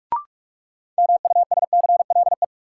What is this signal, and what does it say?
Morse code, heard as a keyed tone, about 46 wpm, 700 Hz. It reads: MUSCLE
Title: Sentences from Top 1300 Words - New Word Drill - 45wpm